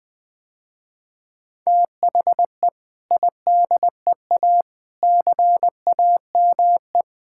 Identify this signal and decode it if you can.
Morse code, heard as a keyed tone, about 20 wpm, 700 Hz. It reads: THE IDEA CAME
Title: Sentences with 2-Word Phrases - Spoken Beginning - 20wpm